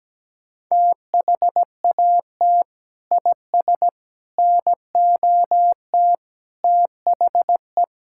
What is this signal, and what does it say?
Morse code, heard as a keyed tone, about 17 wpm, 700 Hz. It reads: THAT IS NOT THE
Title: Sentences from Top 1000 Words - Review Part 5 - 17wpm